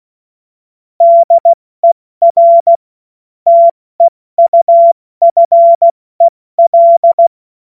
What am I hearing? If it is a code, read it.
Morse code, heard as a keyed tone, about 16 wpm, spaced as for 14 wpm, 680 Hz. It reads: DER TEUFEL
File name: Morsekurs / Witze / Witz 09. August 2022 70 F80